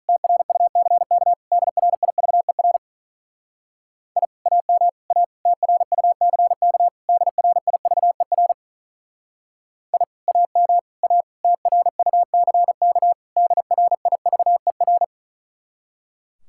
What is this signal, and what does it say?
Morse code, heard as a keyed tone, about 44 wpm, 700 Hz. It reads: TRUCK DRIVER I AM A TRUCK DRIVER I AM A TRUCK DRIVER
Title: Speed-Racing Common QSO Phrases - 35wpm